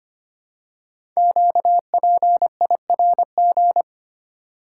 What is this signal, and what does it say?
Morse code, heard as a keyed tone, about 25 wpm, 700 Hz. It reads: QPIRG
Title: Code Groups - 5 Letters - 25wpm